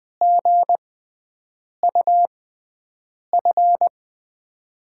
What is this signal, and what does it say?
Morse code, heard as a keyed tone, about 20 wpm, spaced as for 7 wpm, 700 Hz. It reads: GUF
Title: USA Call Signs - 6 Characters - 6x Character Spacing - 20wpm